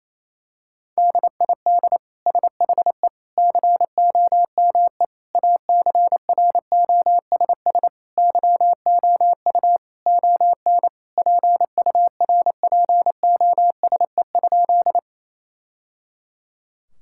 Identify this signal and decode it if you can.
Morse code, heard as a keyed tone, about 28 wpm, 700 Hz. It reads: DID SHE COME ACROSS YOU ON PURPOSE?
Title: Sentences from Top 1000 Words - Review Part 5 - 28wpm